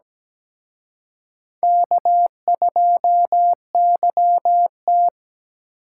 Morse code, heard as a keyed tone, about 20 wpm, 700 Hz. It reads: K2YT
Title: Speed Racing - USA 1x2 Call Signs 17wpm